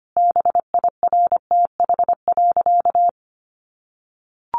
Morse code, heard as a keyed tone, about 25 wpm, 700 Hz. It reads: BIRTH.
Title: Sentences from Top 1300 Words 25wpm